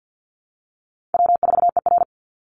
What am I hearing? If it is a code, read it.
Morse code, heard as a keyed tone, about 50 wpm, 700 Hz. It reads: W4ER
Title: Easy Call Signs Repeated 3x - 50wpm